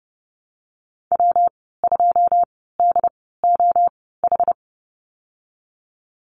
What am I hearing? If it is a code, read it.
Morse code, heard as a keyed tone, about 30 wpm, spaced as for 17 wpm, 700 Hz. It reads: W2DOH
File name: Lesson 19g - Letter w - Call Signs 30wpm